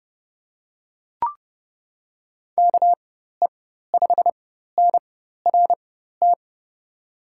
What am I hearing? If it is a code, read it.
Morse code, heard as a keyed tone, about 30 wpm, spaced as for 14 wpm, 700 Hz. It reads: KE5NRT
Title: USA Call Signs - 6 Characters - 4x Character Spacing - 30wpm